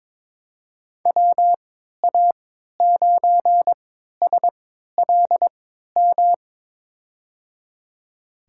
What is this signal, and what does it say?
Morse code, heard as a keyed tone, about 22 wpm, spaced as for 13 wpm, 700 Hz. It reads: WA9SLM
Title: USA Call Signs - 6 Characters - 3x Character Spacing - 22wpm